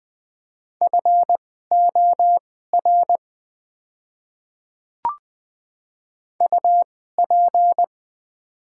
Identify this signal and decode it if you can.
Morse code, heard as a keyed tone, about 20 wpm, spaced as for 14 wpm, 690 Hz. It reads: FOR UP
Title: 2x Character Spacing - Top 200 Words - 20wpm